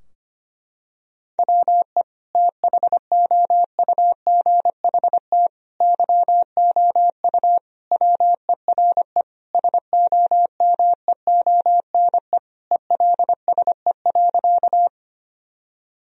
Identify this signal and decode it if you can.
Morse code, heard as a keyed tone, about 25 wpm, 700 Hz. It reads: WE THOUGHT YOU WERE SOMEONE ELSE.
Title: Sentences from Top 1000 Words - Review Part 4 - 25wpm